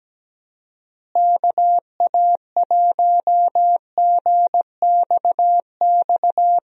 Morse code, heard as a keyed tone, about 17 wpm, 700 Hz. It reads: KA1GXX
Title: USA Call Signs - 6 Characters - Repeated 2x - 17wpm